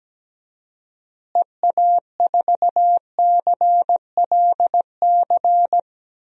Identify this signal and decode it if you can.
Morse code, heard as a keyed tone, about 17 wpm, 700 Hz. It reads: EA4CLC
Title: DXCC-100 Call Signs - 6 Characters - Repeated 2x - 17wpm